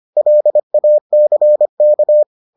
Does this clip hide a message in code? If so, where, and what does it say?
Yes: Morse code from 0.2 to 2.2 s, heard as a keyed tone, about 25 wpm, 600 Hz. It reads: LACK